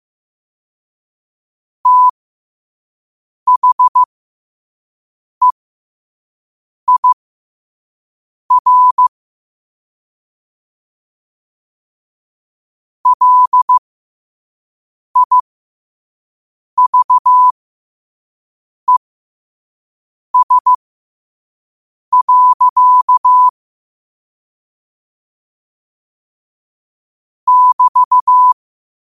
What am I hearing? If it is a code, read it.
Morse code, heard as a keyed tone, about 15 wpm, spaced as for 5 wpm, 1000 Hz. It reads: THEIR LIVES. <BT>
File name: 5wpm